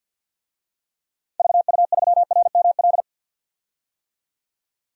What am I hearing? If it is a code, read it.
Morse code, heard as a keyed tone, about 52 wpm, 700 Hz. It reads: VU3RML
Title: DXCC-100 Call Signs - 6 Characters - 50wpm